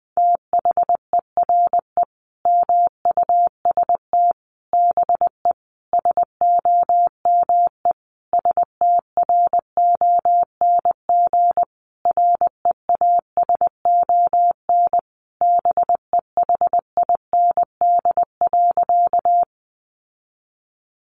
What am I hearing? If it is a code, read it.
Morse code, heard as a keyed tone, about 20 wpm, 700 Hz. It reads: THERE MUST BE SOME STRONG REASON BEHIND.
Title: Sentences from Top 700 Words 20wpm